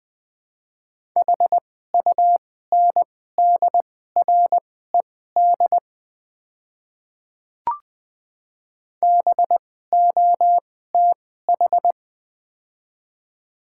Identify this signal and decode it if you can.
Morse code, heard as a keyed tone, about 20 wpm, spaced as for 15 wpm, 700 Hz. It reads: HUNDRED BOTH
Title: 2x Character Spacing - Top 500 Words - 20wpm